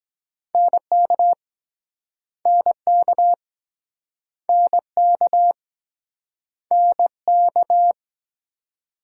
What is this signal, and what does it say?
Morse code, heard as a keyed tone, about 23 wpm, 700 Hz. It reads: NK NK NK NK